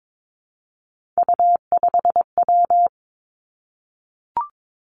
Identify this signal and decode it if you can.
Morse code, heard as a keyed tone, about 22 wpm, 700 Hz. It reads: U5W
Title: Three Challenging Letter-Number Combinations 22wpm